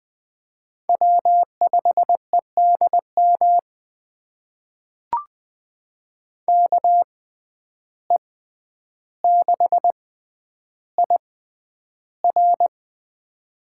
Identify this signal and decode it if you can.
Morse code, heard as a keyed tone, about 21 wpm, spaced as for 10 wpm, 700 Hz. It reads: W5EDM KE6IR
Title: USA Call Signs - 5 Characters - 6x Character Spacing - 20wpm